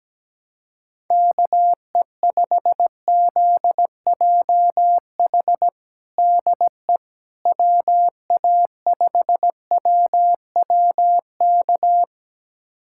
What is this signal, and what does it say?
Morse code, heard as a keyed tone, about 17 wpm, 700 Hz. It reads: KE5ZJH DE WA5WWK
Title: USA Callsign DE Callsign - 6 Characters - 17wpm